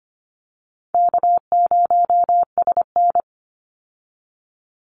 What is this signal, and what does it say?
Morse code, heard as a keyed tone, about 25 wpm, 700 Hz. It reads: K0SN
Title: Call Signs Repeated 3x - 25wpm